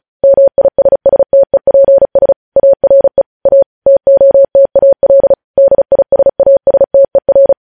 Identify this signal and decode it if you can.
Morse code, heard as a keyed tone, about 36 wpm, 550 Hz. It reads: MISSTEPS ARE A TOTAL DISASTER
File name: podcast / News headlines-1x-35x35